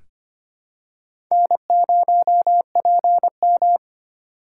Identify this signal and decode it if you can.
Morse code, heard as a keyed tone, about 25 wpm, 700 Hz. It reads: N0PM
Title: USA Call Signs - 4x Character Spacing - 25wpm